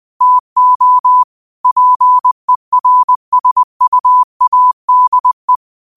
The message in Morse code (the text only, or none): TO PERSUADE